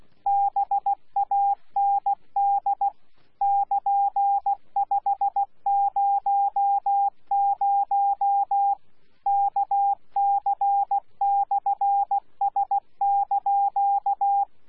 Morse code, heard as a keyed tone, about 16 wpm, 800 Hz. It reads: BAND <KN>500 KC/S)
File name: Z PI4VRZ / PI4VRZ 14-jun-2025